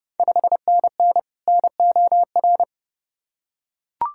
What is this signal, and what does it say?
Morse code, heard as a keyed tone, about 32 wpm, 690 Hz. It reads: 5NN NOR
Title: Ontario QSO Party Exchange 30wpm